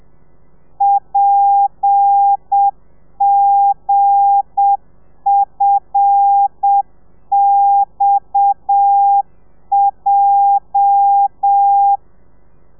Morse code, heard as a keyed tone, about 7 wpm, 800 Hz. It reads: PGFXJ
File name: telegrafia / i2tfj / Lez8 35